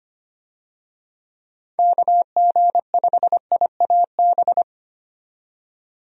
Morse code, heard as a keyed tone, about 25 wpm, 700 Hz. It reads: KG5IAB
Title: USA Call Signs - 6 Characters - 25wpm